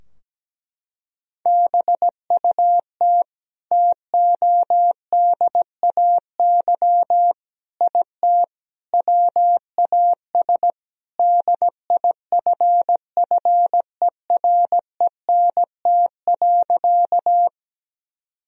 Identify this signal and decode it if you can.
Morse code, heard as a keyed tone, about 17 wpm, 700 Hz. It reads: BUT TODAY IT WAS DIFFERENT.
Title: Sentences from Top 1000 Words - Review Part 4 - 17wpm